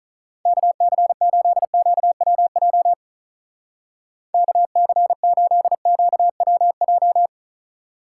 Morse code, heard as a keyed tone, about 46 wpm, 700 Hz. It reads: KC8QWJ KC8QWJ